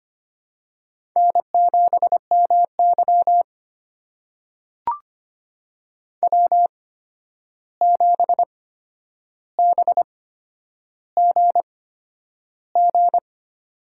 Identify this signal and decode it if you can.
Morse code, heard as a keyed tone, about 25 wpm, spaced as for 10 wpm, 700 Hz. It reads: N7MY W7BGG
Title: USA Call Signs - 8x Character Spacing - 25wpm